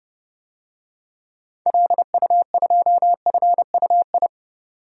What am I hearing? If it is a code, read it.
Morse code, heard as a keyed tone, about 32 wpm, 700 Hz. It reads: LU2FUI
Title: DXCC-100 Call Signs - 6 Characters - 30wpm